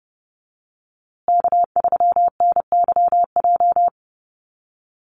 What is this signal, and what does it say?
Morse code, heard as a keyed tone, about 30 wpm, 700 Hz. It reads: K3NYJ
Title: USA Call Signs Repeated 3x 30wpm